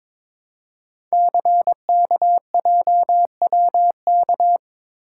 Morse code, heard as a keyed tone, about 22 wpm, 700 Hz. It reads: CKJWK
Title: Code Groups - 5 Letters - 22wpm